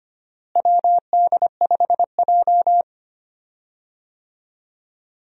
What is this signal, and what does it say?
Morse code, heard as a keyed tone, about 25 wpm, 700 Hz. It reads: WD5J
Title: USA Call Signs - 4 Characters - Repeated 2x - 25wpm